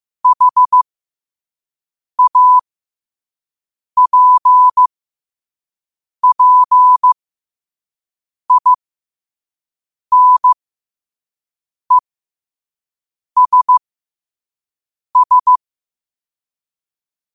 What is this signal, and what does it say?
Morse code, heard as a keyed tone, about 15 wpm, spaced as for 5 wpm, 1000 Hz. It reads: HAPPINESS
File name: qotdfiles / 5wpm